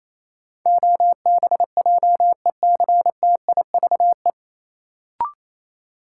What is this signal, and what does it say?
Morse code, heard as a keyed tone, about 28 wpm, 700 Hz. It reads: OBJECTIVE
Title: Sentences from Top 1200 Words - New Word Drill Alt - 28wpm